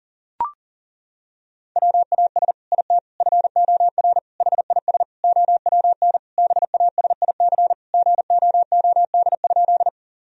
Morse code, heard as a keyed tone, about 37 wpm, 700 Hz. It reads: WAS IT FOR HIS OWN BASIC GOOD?
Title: Sentences from Top 1000 Words - Review Part 3 - 40wpm